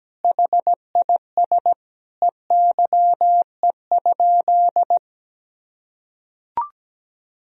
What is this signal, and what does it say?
Morse code, heard as a keyed tone, about 17 wpm, 700 Hz. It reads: HIS EYE?
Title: Sentences from Top 300 Words and Abbreviations 17wpm